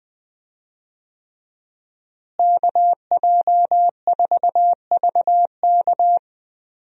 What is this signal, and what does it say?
Morse code, heard as a keyed tone, about 21 wpm, 700 Hz. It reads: KJ4VK
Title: USA Call Signs - 5 Characters - Repeated 3x - 20wpm